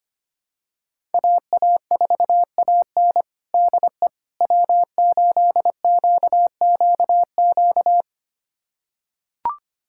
Morse code, heard as a keyed tone, about 25 wpm, 700 Hz. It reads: AA4AN DE W8QQQ
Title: USA Callsign DE Callsign 25wpm